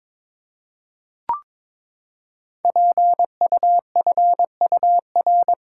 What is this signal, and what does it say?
Morse code, heard as a keyed tone, about 22 wpm, 700 Hz. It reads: PUFUR
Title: Code Groups - 5 Letters - 22wpm